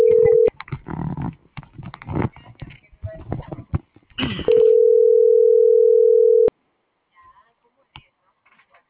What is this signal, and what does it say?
Ringback tone (North America), 440 + 480 Hz.
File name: CENTRAL GENERAL / out alexander.sanchez